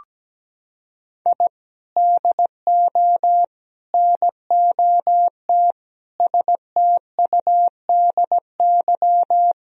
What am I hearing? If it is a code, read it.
Morse code, heard as a keyed tone, about 17 wpm, 700 Hz. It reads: I DO NOT STUDY
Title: Sentences from Top 1000 Words - Review Part 1 - 17wpm